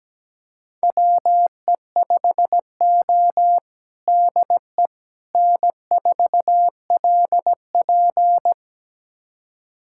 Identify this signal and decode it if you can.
Morse code, heard as a keyed tone, about 17 wpm, 700 Hz. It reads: WE5O DE N4LP
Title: USA Callsign DE Callsign - 4 Characters - 17wpm